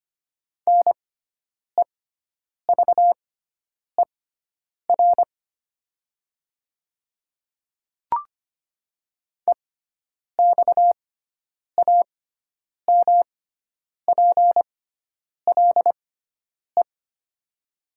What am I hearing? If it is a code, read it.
Morse code, heard as a keyed tone, about 25 wpm, spaced as for 9 wpm, 700 Hz. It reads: NEVER EXAMPLE
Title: 6x Character Spacing - Top 300 Words - 25wpm